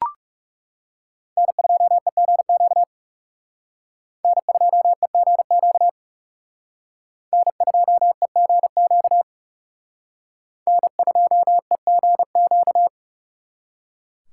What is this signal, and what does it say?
Morse code, heard as a keyed tone, about 38 wpm, 700 Hz. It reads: N2EGQ N2EGQ N2EGQ N2EGQ
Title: Speed Racing - USA 1x3 Call Signs 30wpm